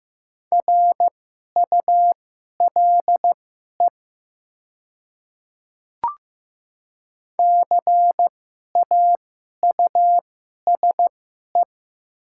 Morse code, heard as a keyed tone, about 15 wpm, spaced as for 11 wpm, 700 Hz. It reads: RULE CAUSE
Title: 2x Character Spacing - Top 500 Words - 15wpm